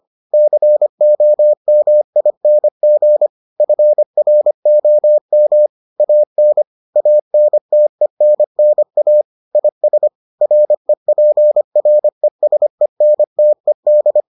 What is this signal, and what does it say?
Morse code, heard as a keyed tone, about 25 wpm, 600 Hz. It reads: COMING FROM AN ANTENNA IS REPRESENTED